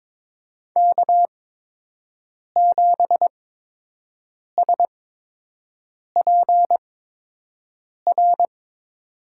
Morse code, heard as a keyed tone, about 21 wpm, spaced as for 6 wpm, 700 Hz. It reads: K7SPR
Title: USA Call Signs - 5 Characters - 8x Character Spacing - 22wpm